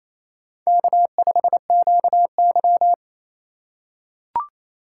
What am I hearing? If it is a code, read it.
Morse code, heard as a keyed tone, about 28 wpm, 700 Hz. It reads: K5QY
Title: USA Call Signs - 4 Characters - 3x Character Spacing - 28wpm